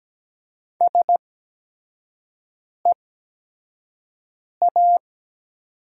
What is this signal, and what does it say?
Morse code, heard as a keyed tone, about 17 wpm, spaced as for 5 wpm, 700 Hz. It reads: SEA
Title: 8x Character Spacing - Top 500 Words - 17wpm